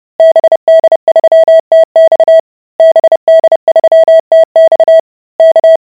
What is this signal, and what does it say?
Morse code, heard as a keyed tone, about 30 wpm, 650 Hz. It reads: BD3TX BD3TX K